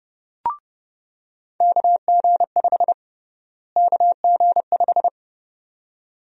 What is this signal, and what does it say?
Morse code, heard as a keyed tone, about 30 wpm, 700 Hz. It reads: KG5 KG5
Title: USA Call Sign Prefixes - Repeated 2x - 30wpm